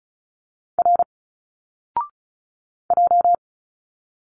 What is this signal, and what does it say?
Morse code, heard as a keyed tone, about 36 wpm, 700 Hz. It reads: R J
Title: Single Letters 35wpm